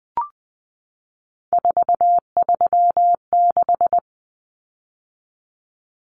Morse code, heard as a keyed tone, about 20 wpm, 700 Hz. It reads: 436
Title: Common Cut Numbers - 3-Digit 20wpm